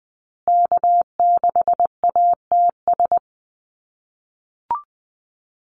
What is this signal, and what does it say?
Morse code, heard as a keyed tone, about 20 wpm, 700 Hz. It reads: K6ATS
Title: USA Call Signs Repeated 2x 20wpm